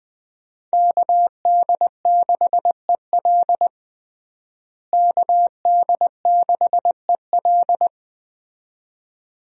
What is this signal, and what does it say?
Morse code, heard as a keyed tone, about 20 wpm, 700 Hz. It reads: KD6EL KD6EL
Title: USA Call Signs - 5 Characters - Repeated 2x - 20wpm